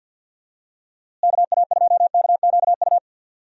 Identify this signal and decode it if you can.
Morse code, heard as a keyed tone, about 46 wpm, 700 Hz. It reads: KA2KQU
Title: USA Call Signs - 6 Characters - Repeated 2x - 50wpm